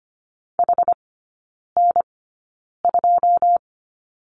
Morse code, heard as a keyed tone, about 25 wpm, spaced as for 9 wpm, 690 Hz. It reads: HN2
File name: Lesson 14e - Number 2 - Three Characters 25wpm